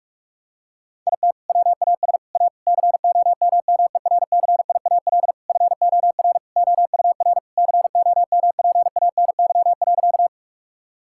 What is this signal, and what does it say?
Morse code, heard as a keyed tone, about 46 wpm, 700 Hz. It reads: IT WAS A COMMERCIAL FOR OUR COMPANY.
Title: Sentences from Top 1000 Words - Review Part 5 - 45wpm